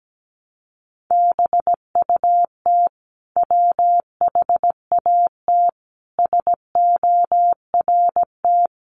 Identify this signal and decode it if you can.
Morse code, heard as a keyed tone, about 17 wpm, 700 Hz. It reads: BUT WHAT SORT
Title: Sentences from Top 1300 Words 17wpm